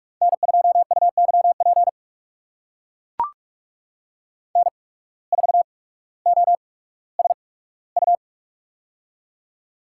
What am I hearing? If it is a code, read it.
Morse code, heard as a keyed tone, about 44 wpm, spaced as for 19 wpm, 700 Hz. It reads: N2UYP N4OSU
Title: USA Call Signs - 5 Characters - 8x Character Spacing - 45wpm